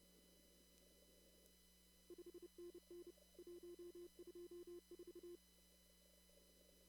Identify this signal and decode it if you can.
Morse code, heard as a keyed tone, about 30 wpm, 350 Hz. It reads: 5NN 124
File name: CW / kt5j wpx cw-op-k5pi